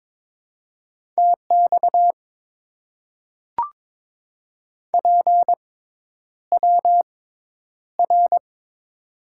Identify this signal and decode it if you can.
Morse code, heard as a keyed tone, about 22 wpm, spaced as for 10 wpm, 700 Hz. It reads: TX PWR
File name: Top 90 QSO Elements - 6x Character Spacing - 22wpm